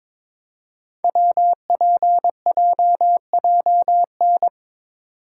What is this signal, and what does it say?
Morse code, heard as a keyed tone, about 22 wpm, 700 Hz. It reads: WPJJN
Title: Code Groups - 5 Letters - 22wpm